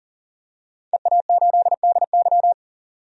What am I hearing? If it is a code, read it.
Morse code, heard as a keyed tone, about 40 wpm, 700 Hz. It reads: EA8DY
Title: DXCC-100 Call Signs - 5 Characters - 40wpm